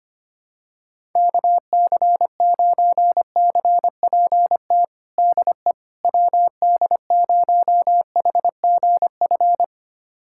Morse code, heard as a keyed tone, about 25 wpm, 700 Hz. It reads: KC9CPT DE WD0HGF
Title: USA Callsign DE Callsign - 6 Characters - 25wpm